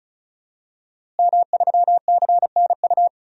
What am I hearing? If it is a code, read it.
Morse code, heard as a keyed tone, about 34 wpm, 700 Hz. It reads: M3CNU
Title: DXCC-100 Call Signs - 5 Characters - Repeated 3x - 35wpm